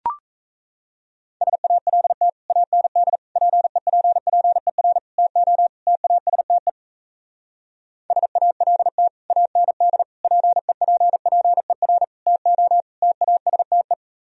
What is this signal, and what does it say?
Morse code, heard as a keyed tone, about 34 wpm, 700 Hz. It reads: SALT AND PEPPER TO TASTE SALT AND PEPPER TO TASTE
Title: Speed-Racing - Sets of 5 Words 28wpm